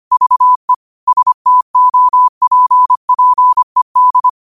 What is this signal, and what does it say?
Morse code, heard as a keyed tone, about 25 wpm, 1000 Hz. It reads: VE STOPPED
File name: qotdfiles / 25wpm